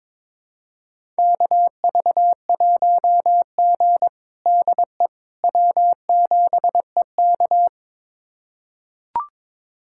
Morse code, heard as a keyed tone, about 22 wpm, 700 Hz. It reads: KV1G DE W7EK
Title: USA Callsign DE Callsign - 4 Characters - 22wpm